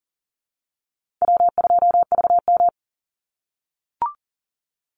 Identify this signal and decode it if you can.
Morse code, heard as a keyed tone, about 40 wpm, 700 Hz. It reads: W2VM
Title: Easy Call Signs - 40wpm